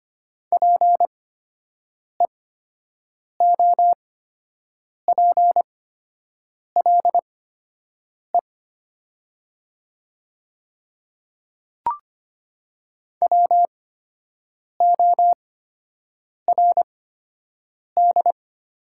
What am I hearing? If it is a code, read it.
Morse code, heard as a keyed tone, about 25 wpm, spaced as for 7 wpm, 700 Hz. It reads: PEOPLE WORD
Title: 8x Character Spacing - Top 100 Words - 25wpm